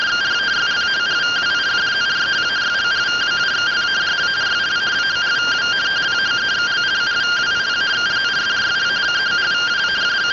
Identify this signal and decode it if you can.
Two-tone teleprinter signal, about 170 Hz shift.
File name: findings / axial iodine 10mm 184.gen-22 rtty rx